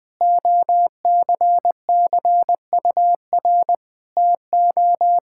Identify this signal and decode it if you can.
Morse code, heard as a keyed tone, about 20 wpm, 700 Hz. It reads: OCCUR TO
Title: Sentences from Top 1000 Words - Review Part 6 - 20wpm